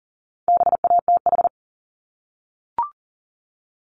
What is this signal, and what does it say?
Morse code, heard as a keyed tone, about 39 wpm, 700 Hz. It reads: BATH
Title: Multi-letter Phonemes Repeated 3x 40wpm